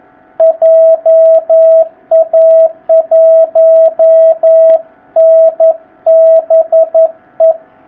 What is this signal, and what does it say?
Morse code, heard as a keyed tone, about 11 wpm, 650 Hz, with radio noise. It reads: JA1NBE
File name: jh8cbh / 2019cw